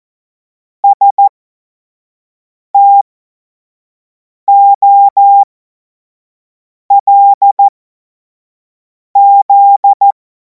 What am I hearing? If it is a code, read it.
Morse code, heard as a keyed tone, about 14 wpm, spaced as for 5 wpm, 800 Hz. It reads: STOLZ
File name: Morsekurs / Witze / Witz 25. August 2026 25 F70